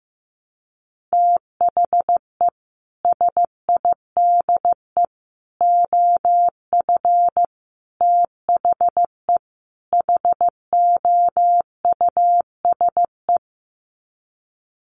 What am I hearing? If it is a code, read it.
Morse code, heard as a keyed tone, about 15 wpm, 700 Hz. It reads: THE SIDE OF THE HOUSE
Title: Sets of 5 Words Encore2 15wpm